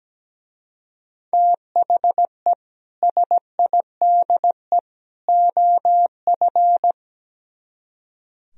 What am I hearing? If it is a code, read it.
Morse code, heard as a keyed tone, about 17 wpm, 700 Hz. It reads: THE SIDE OF
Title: Speed-Racing - Sets of 3 Words 17wpm